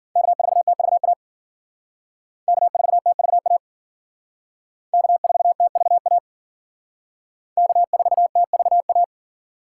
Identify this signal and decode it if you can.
Morse code, heard as a keyed tone, about 53 wpm, 700 Hz. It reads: K4TVA K4TVA K4TVA K4TVA